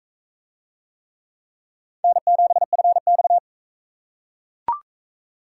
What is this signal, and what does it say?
Morse code, heard as a keyed tone, about 40 wpm, 700 Hz. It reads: N7FX